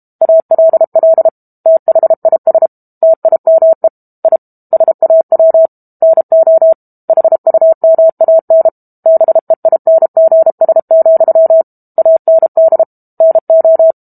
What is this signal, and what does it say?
Morse code, heard as a keyed tone, about 32 wpm, 650 Hz. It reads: ALL THIS TIME I SAW NO HUMAN BEINGS, AND NO
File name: Book Two / 06-Chapter 6 - 32 WPM